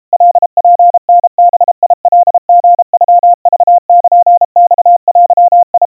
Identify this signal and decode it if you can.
Morse code (Japanese Wabun code), heard as a keyed tone, about 32 wpm, 700 Hz. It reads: カツタバカリノクルマデ
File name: kawa / Morse sample / kamyu 120 cpm 700 Hz Crispy